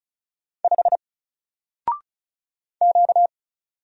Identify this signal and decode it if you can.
Morse code, heard as a keyed tone, about 36 wpm, 700 Hz. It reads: 5 Q
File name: Single-Letter-Number-35wpm